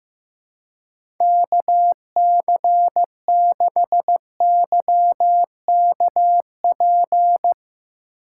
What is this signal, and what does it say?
Morse code, heard as a keyed tone, about 15 wpm, 700 Hz. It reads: KC6YKP
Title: USA Call Signs - 6 Characters - 3x Character Spacing - 15wpm